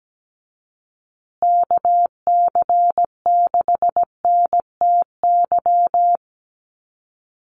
Morse code, heard as a keyed tone, about 17 wpm, 700 Hz. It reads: KC6NTY
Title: USA Call Signs Repeated 2x 17wpm